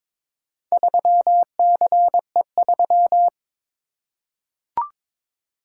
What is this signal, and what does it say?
Morse code, heard as a keyed tone, about 22 wpm, 700 Hz. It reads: 3CE3
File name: practice / 22wpm / Four Letter-Number Combinations - 22wpm